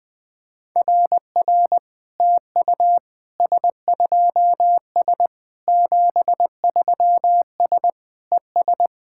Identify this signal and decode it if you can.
Morse code, heard as a keyed tone, about 20 wpm, 700 Hz. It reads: RR TU S2S 73S ES
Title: SOTA QSO Practice 20wpm